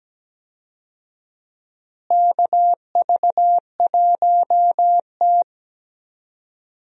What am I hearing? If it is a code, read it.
Morse code, heard as a keyed tone, about 17 wpm, 700 Hz. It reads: KV1T
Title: USA Call Signs - 4 Characters - Repeated 3x - 17wpm